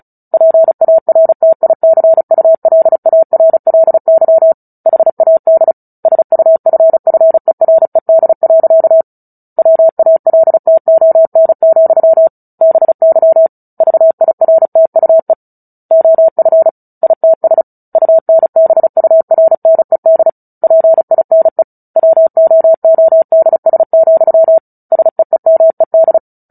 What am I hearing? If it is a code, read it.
Morse code, heard as a keyed tone, about 34 wpm, 650 Hz. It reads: PARTICULARLY HAD SUFFERED. WALTON, BY VIRTUE OF ITS UNBURNED PINE WOODS, SEEMED
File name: Book Two / 09-Chapter 9 - 35 WPM